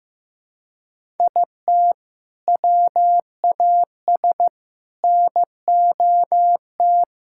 Morse code, heard as a keyed tone, about 15 wpm, 700 Hz. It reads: IT WAS NOT
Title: Sentences from Top 1000 Words - Review Part 6 - 15wpm